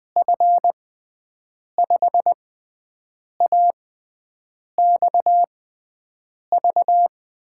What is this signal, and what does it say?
Morse code, heard as a keyed tone, about 20 wpm, spaced as for 7 wpm, 700 Hz. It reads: F5AXV
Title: USA Call Signs - 6 Characters - 6x Character Spacing - 20wpm